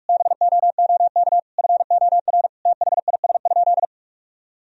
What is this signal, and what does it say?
Morse code, heard as a keyed tone, about 46 wpm, 700 Hz. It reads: BOOK FOR THIS?
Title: Sentences from Top 1000 Words - Review Part 4 - 45wpm